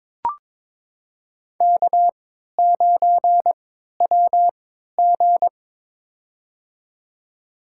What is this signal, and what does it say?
Morse code, heard as a keyed tone, about 22 wpm, spaced as for 13 wpm, 700 Hz. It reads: K9WG
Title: USA Call Signs - 4 Characters - 3x Character Spacing - 22wpm